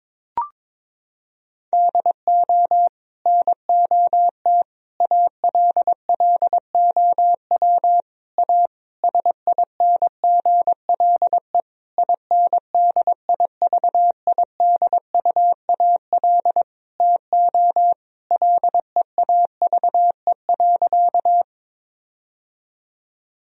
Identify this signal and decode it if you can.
Morse code, heard as a keyed tone, about 22 wpm, 700 Hz. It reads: DO NOT ALLOW A SINGLE INDIVIDUAL TO LEAVE.
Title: Sentences from Top 1000 Words - Review Part 4 - 22wpm